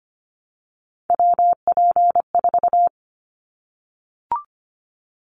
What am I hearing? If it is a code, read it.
Morse code, heard as a keyed tone, about 25 wpm, 700 Hz. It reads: WP4
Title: Callsign Prefixes for Top 25 DXCC - 25wpm